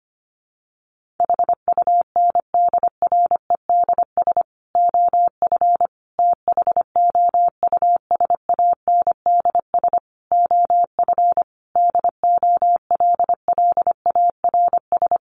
Morse code, heard as a keyed tone, about 25 wpm, 700 Hz. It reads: HUNDREDS OF THOUSANDS OF DOLLARS
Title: Sets of 5 Words Encore 25wpm